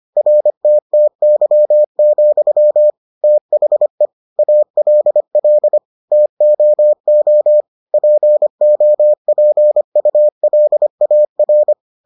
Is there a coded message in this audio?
Yes: Morse code from 0.2 to 11.7 s, heard as a keyed tone, about 25 wpm, 600 Hz. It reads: RTTY, THE ALL TOO POPULAR